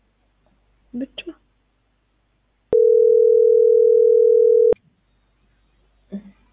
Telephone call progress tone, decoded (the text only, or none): ringback tone